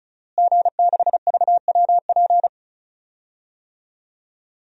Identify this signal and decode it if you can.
Morse code, heard as a keyed tone, about 35 wpm, 700 Hz. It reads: G6VWP
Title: DXCC-100 Call Signs - 5 Characters - Repeated 2x - 35wpm